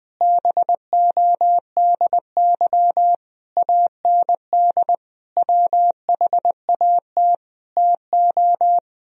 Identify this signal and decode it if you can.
Morse code, heard as a keyed tone, about 21 wpm, 700 Hz. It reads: BODY AND WHAT TO
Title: Sentences from Top 1000 Words - Review Part 2 - 20wpm